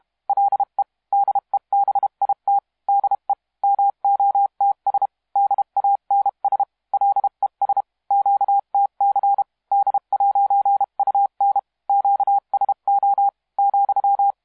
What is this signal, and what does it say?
Morse code, heard as a keyed tone, about 32 wpm, 800 Hz. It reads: LE DEBIT DE MOTS DANS LES QTC D'UN QSO ,
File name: musiqueradio / cw hsc demo f6ddr